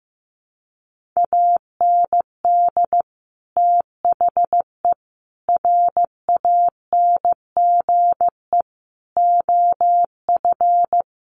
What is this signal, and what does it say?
Morse code, heard as a keyed tone, about 15 wpm, 700 Hz. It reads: AND THE RANGE OF
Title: Sentences from Top 1500 Words 15wpm